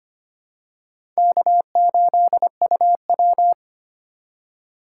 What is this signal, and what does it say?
Morse code, heard as a keyed tone, about 25 wpm, 700 Hz. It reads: K8UW
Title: USA Call Signs - 3x Character Spacing - 25wpm